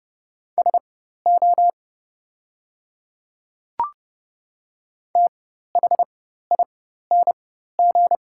Morse code, heard as a keyed tone, about 30 wpm, spaced as for 14 wpm, 700 Hz. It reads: SO THING
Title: 4x Character Spacing - Top 100 Words - 30wpm